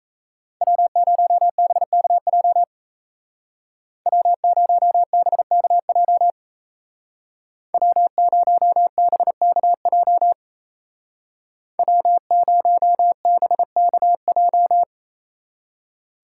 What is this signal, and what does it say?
Morse code, heard as a keyed tone, about 34 wpm, 700 Hz. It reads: W0BKJ W0BKJ W0BKJ W0BKJ
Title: Speed Racing - USA 1x3 Call Signs 28wpm